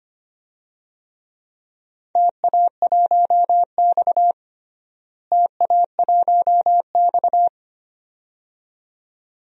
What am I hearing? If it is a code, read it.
Morse code, heard as a keyed tone, about 25 wpm, 700 Hz. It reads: TA1X TA1X
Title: DXCC-100 Call Signs - 4 Characters - Repeated 2x - 25wpm